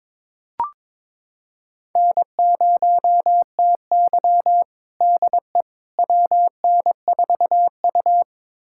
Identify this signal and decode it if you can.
Morse code, heard as a keyed tone, about 22 wpm, 700 Hz. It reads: N0TY DE WN4U
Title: USA Callsign DE Callsign - 4 Characters - 22wpm